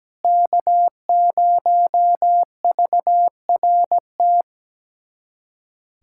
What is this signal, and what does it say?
Morse code, heard as a keyed tone, about 17 wpm, 700 Hz. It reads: K0VRT
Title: USA Callsign DE Callsign - 5 Characters - 17wpm